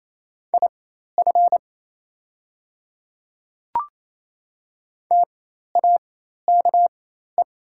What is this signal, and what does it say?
Morse code, heard as a keyed tone, about 28 wpm, spaced as for 13 wpm, 700 Hz. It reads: IF TAKE
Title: 4x Character Spacing - Top 300 Words - 28wpm